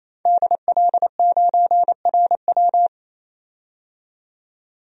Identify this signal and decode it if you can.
Morse code, heard as a keyed tone, about 28 wpm, 700 Hz. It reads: DL9RW
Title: DXCC-100 Call Signs - 5 Characters - 28wpm